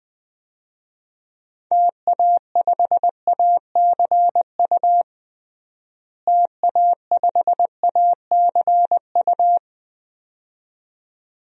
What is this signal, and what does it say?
Morse code, heard as a keyed tone, about 20 wpm, 700 Hz. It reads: TA5ACU TA5ACU
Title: DXCC-100 Call Signs - 6 Characters - Repeated 2x - 20wpm